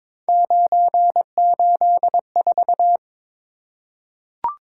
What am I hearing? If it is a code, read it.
Morse code, heard as a keyed tone, about 21 wpm, 700 Hz. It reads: 984
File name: practice / 22wpm / Three Letter-Number Combinations - 22wpm